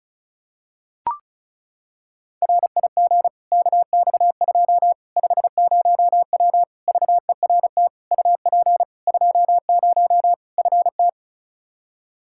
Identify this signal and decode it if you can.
Morse code, heard as a keyed tone, about 36 wpm, 700 Hz. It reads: RIG KX2 50W VERT UP 20 FT
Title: Practice QSO - Exchange 2 - Info - 35wpm